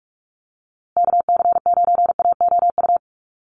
Morse code, heard as a keyed tone, about 48 wpm, 700 Hz. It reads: KC9AOU
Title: USA Call Signs Repeated 3x 45wpm